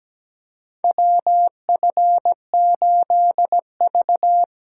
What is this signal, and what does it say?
Morse code, heard as a keyed tone, about 17 wpm, 700 Hz. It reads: WF8V
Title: USA Call Signs - 4 Characters - 6x Character Spacing - 17wpm